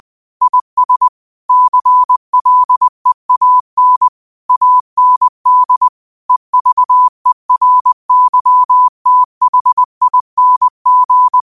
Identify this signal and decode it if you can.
Morse code, heard as a keyed tone, about 20 wpm, 1000 Hz. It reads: IS CLEAN AND EVERYTHING